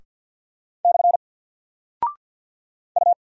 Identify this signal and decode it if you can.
Morse code, heard as a keyed tone, about 50 wpm, 700 Hz. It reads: / U
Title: Single Character (Letter:Number:Punctuation:Prosign) 50wpm